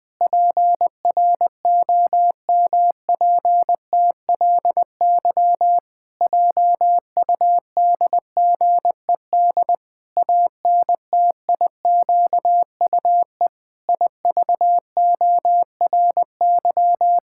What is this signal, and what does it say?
Morse code, heard as a keyed tone, about 21 wpm, 700 Hz. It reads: PROMPTLY JUDGED ANTIQUE IVORY
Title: Pangram Challenge v2 - 20wpm